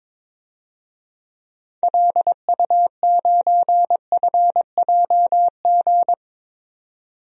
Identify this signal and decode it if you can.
Morse code, heard as a keyed tone, about 21 wpm, 700 Hz. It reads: LU9FJG
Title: DXCC-100 Call Signs - 6 Characters - 22wpm